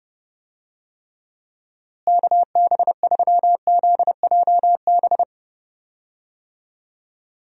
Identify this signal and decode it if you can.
Morse code, heard as a keyed tone, about 30 wpm, 700 Hz. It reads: KB3ZJB
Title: USA Call Signs - 6 Characters - Repeated 3x - 30wpm